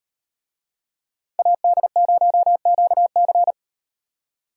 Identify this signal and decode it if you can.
Morse code, heard as a keyed tone, about 37 wpm, 700 Hz. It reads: AD0QC